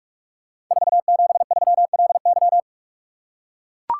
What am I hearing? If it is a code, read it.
Morse code, heard as a keyed tone, about 44 wpm, 690 Hz. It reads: 473LY
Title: Code Groups - 5 Letter:Numbers - 45wpm